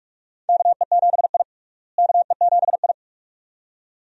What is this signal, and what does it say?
Morse code, heard as a keyed tone, about 46 wpm, 700 Hz. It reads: KE7I KE7I
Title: USA Call Signs - 4 Characters - Repeated 2x - 45wpm